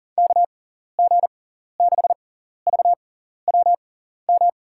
Morse code, heard as a keyed tone, about 38 wpm, spaced as for 14 wpm, 700 Hz. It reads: KG6VWM
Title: USA Call Signs - 6 Characters - 6x Character Spacing - 40wpm